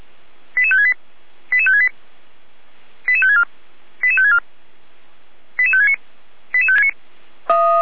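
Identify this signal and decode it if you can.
Five-tone selective call (ZVEI), heard as quick runs of five tones, tones of about 70 ms. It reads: 88568 88568 88564 88564 88569
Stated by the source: ZVEI five-tone call 88133 received on SDR.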